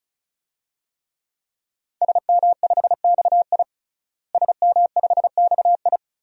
Morse code, heard as a keyed tone, about 34 wpm, 700 Hz. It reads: SM5XI SM5XI
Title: DXCC-100 Call Signs - 5 Characters - Repeated 2x - 35wpm